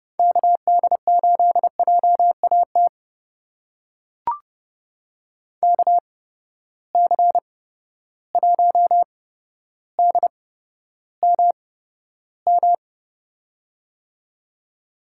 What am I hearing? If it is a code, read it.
Morse code, heard as a keyed tone, about 30 wpm, spaced as for 13 wpm, 700 Hz. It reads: KD8JAT KC1DMM
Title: USA Call Signs - 6 Characters - 8x Character Spacing - 30wpm